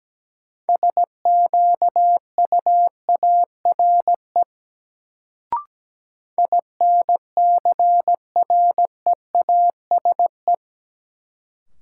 Morse code, heard as a keyed tone, about 17 wpm, 700 Hz. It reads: SQUARE INCREASE
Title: Sentences from Top 1200 Words - New Word Drill Alt - 17wpm